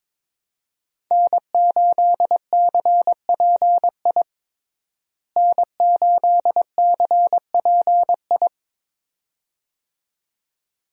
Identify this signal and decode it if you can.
Morse code, heard as a keyed tone, about 22 wpm, 700 Hz. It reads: N8CPI N8CPI
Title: USA Call Signs - 5 Characters - Repeated 2x - 22wpm